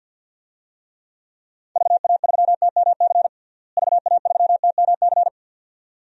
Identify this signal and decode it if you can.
Morse code, heard as a keyed tone, about 47 wpm, 700 Hz. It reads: VA3TMC VA3TMC
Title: DXCC-100 Call Signs - 6 Characters - Repeated 2x - 50wpm